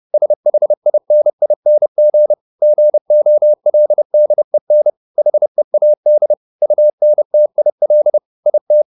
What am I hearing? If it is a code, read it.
Morse code, heard as a keyed tone, about 30 wpm, 600 Hz. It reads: SHINING GOLDEN HEAD UNTIL IT